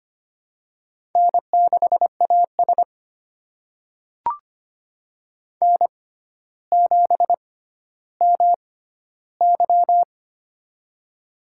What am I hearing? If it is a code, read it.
Morse code, heard as a keyed tone, about 25 wpm, spaced as for 13 wpm, 700 Hz. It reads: N6AS N7MY
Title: USA Call Signs - 4 Characters - 6x Character Spacing - 25wpm